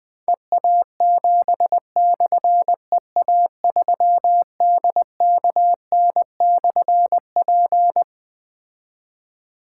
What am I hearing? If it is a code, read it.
Morse code, heard as a keyed tone, about 20 wpm, 700 Hz. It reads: EA7/EA3DKN/P
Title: Difficult Call Signs 20wpm